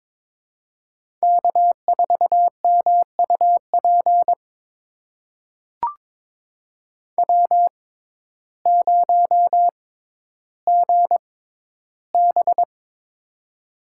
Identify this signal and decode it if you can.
Morse code, heard as a keyed tone, about 22 wpm, spaced as for 12 wpm, 700 Hz. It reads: K4MUP W0GB
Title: USA Call Signs - 6x Character Spacing - 22wpm